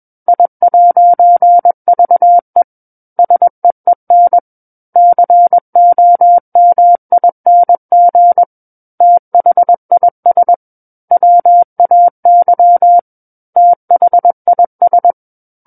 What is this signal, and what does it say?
Morse code, heard as a keyed tone, about 21 wpm, 700 Hz. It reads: I'VE SEEN COMING THIS WAY THIS
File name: Book One / 12-Chapter 12 - 21 WPM